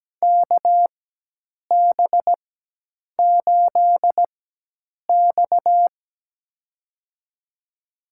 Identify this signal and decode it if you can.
Morse code, heard as a keyed tone, about 17 wpm, spaced as for 8 wpm, 700 Hz. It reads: KB8X
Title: USA Call Signs - 4 Characters - 4x Character Spacing - 17wpm